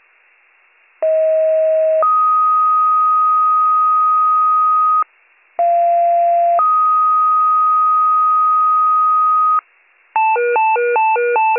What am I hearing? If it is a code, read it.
Two-tone page, heard as a one-second tone, then a three-second tone, about 640 then 1250 Hz, about 690 then 1250 Hz. It is heard from 1.0 to 9.6 s.